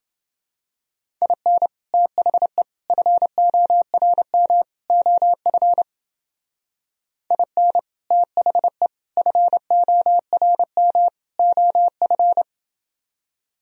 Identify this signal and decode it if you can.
Morse code, heard as a keyed tone, about 28 wpm, 700 Hz. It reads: IN THE FORM OF IN THE FORM OF